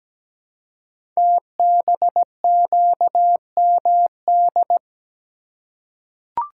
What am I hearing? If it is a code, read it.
Morse code, heard as a keyed tone, about 17 wpm, 700 Hz. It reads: TBQMD
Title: Code Groups - 5 Letters - 17wpm